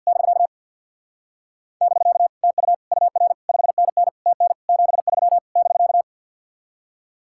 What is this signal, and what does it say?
Morse code, heard as a keyed tone, about 62 wpm, 700 Hz. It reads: <BK> <BK> TU UR 5NN TN 73 <BK>
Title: Speed-Racing Common QSO Phrases Encore - 50wpm